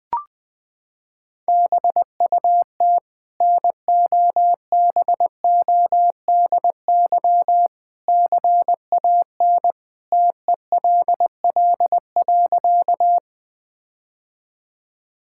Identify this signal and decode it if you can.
Morse code, heard as a keyed tone, about 20 wpm, 700 Hz. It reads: BUT NOBODY CAN TELL.
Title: Sentences from Top 1600 Words 20wpm